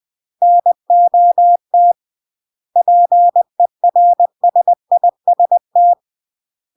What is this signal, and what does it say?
Morse code, heard as a keyed tone, about 20 wpm, 700 Hz. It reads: NOT PERSIST
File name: CW / jodorowsky / 82 commandments 20-20